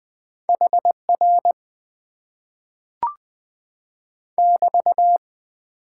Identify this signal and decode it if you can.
Morse code, heard as a keyed tone, about 20 wpm, 700 Hz. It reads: HR <BT>
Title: Top 90 QSO Elements - 6x Character Spacing - 20wpm